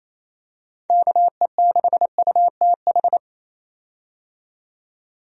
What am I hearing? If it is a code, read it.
Morse code, heard as a keyed tone, about 28 wpm, 700 Hz. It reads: KE6UTH
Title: USA Call Signs - 6 Characters - 28wpm